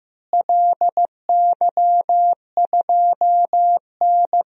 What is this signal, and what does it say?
Morse code, heard as a keyed tone, about 16 wpm, 700 Hz. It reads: LY2N
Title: DXCC-100 Call Signs - 4 Characters - Repeated 2x - 15wpm